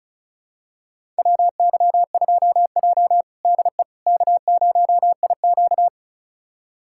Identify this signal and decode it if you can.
Morse code, heard as a keyed tone, about 36 wpm, 700 Hz. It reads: WY2J DE K0IQ
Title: USA Callsign DE Callsign - 4 Characters - 35wpm